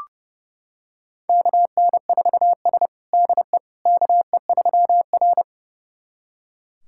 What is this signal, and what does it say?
Morse code, heard as a keyed tone, about 30 wpm, 700 Hz. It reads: KN4S DE KE3R
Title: USA Callsign DE Callsign - 4 Characters - 30wpm